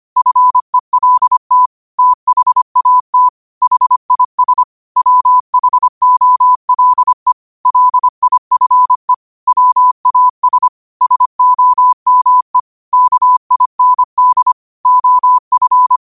Morse code, heard as a keyed tone, about 25 wpm, 1000 Hz. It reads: FELT THAT HIS WHOLE LIFE WAS SOME KIND OF
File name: 25wpm